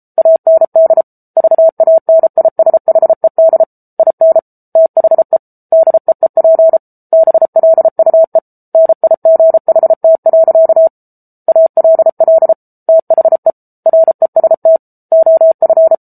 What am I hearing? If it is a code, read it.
Morse code, heard as a keyed tone, about 34 wpm, 650 Hz. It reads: AND VANISHED IN THE DEEP BLUE NIGHT. ALL THE REST OF
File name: Book Two / 07-Chapter 7 - 33 WPM